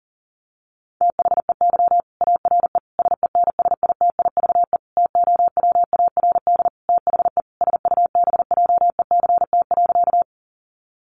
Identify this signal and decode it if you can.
Morse code, heard as a keyed tone, about 40 wpm, 700 Hz. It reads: THEY ARE SENSITIVE TOWARD THE SUBJECT.
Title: Sentences from Top 1400 Words 40wpm